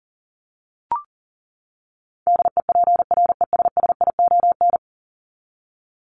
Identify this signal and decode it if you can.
Morse code, heard as a keyed tone, about 40 wpm, 700 Hz. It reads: DEPRESSION
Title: Top 1001-1250 Words 40wpm